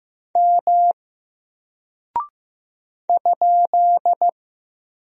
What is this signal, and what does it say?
Morse code, heard as a keyed tone, about 15 wpm, 700 Hz. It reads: M ?
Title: Single Character (Letter:Number:Punctuation:Prosign) 15wpm